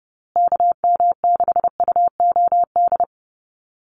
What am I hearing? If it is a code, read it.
Morse code, heard as a keyed tone, about 30 wpm, 700 Hz. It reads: KM6UOD
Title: USA Call Signs Repeated 2x 30wpm